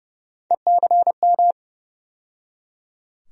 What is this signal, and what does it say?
Morse code, heard as a keyed tone, about 30 wpm, 700 Hz. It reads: ECM
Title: USA Call Sign Postfixes - 3 Letters 30wpm